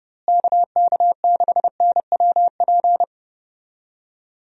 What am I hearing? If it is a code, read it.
Morse code, heard as a keyed tone, about 30 wpm, 700 Hz. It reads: KK6NWP
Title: USA Call Signs - 6 Characters - Repeated 2x - 30wpm